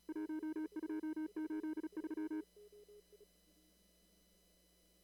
Morse code, heard as a keyed tone, about 35 wpm, 310 Hz. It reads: 1283
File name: CW / kt5j wpx cw-op-k5pi